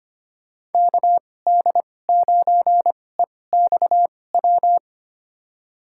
Morse code, heard as a keyed tone, about 25 wpm, spaced as for 18 wpm, 700 Hz. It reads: KD9EXW
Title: USA Call Signs - 2x Character Spacing - 25wpm